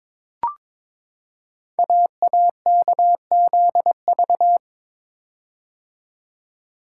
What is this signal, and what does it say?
Morse code, heard as a keyed tone, about 22 wpm, 690 Hz. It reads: AAKZV
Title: Code Groups - 5 Letter:Numbers - 22wpm